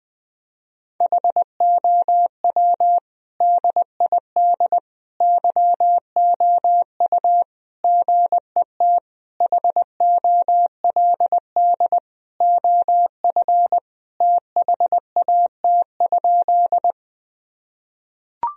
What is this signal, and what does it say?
Morse code, heard as a keyed tone, about 21 wpm, 700 Hz. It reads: HOW DID YOU GET HOLD OF THAT?
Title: Sentences from Top 1000 Words - Review Part 2 - 20wpm